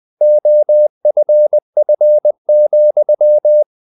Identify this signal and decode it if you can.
Morse code, heard as a keyed tone, about 20 wpm, 600 Hz. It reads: OFF,
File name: the old man and the sea / the old man and the sea part 1 20wpm